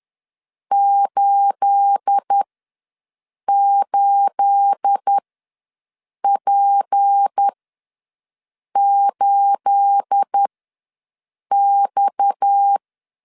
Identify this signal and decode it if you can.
Morse code, heard as a keyed tone, about 10 wpm, spaced as for 6 wpm, 800 Hz. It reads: 88P8X
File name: LEZIONI / 9 28 ws 10wpm 4124